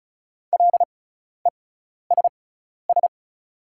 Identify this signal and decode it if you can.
Morse code, heard as a keyed tone, about 34 wpm, spaced as for 12 wpm, 700 Hz. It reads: LESS
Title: 6x Character Spacing - Top 500 Words - 35wpm